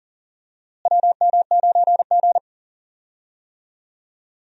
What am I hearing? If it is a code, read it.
Morse code, heard as a keyed tone, about 40 wpm, 700 Hz. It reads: WM9G
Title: USA Call Signs - 4 Characters - Repeated 3x - 40wpm